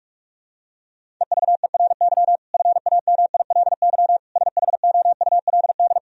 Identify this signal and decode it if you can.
Morse code, heard as a keyed tone, about 48 wpm, 700 Hz. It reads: EVERY FAMILY SHOULD
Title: Sentences from Top 1000 Words - Review Part 1 - 45wpm